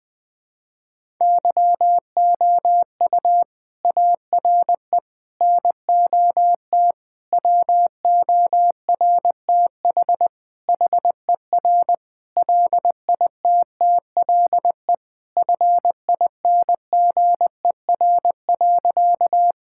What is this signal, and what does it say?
Morse code, heard as a keyed tone, about 20 wpm, 700 Hz. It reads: YOU ARE NOT WORTH HER LITTLE FINGER.
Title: Sentences from Top 1600 Words 20wpm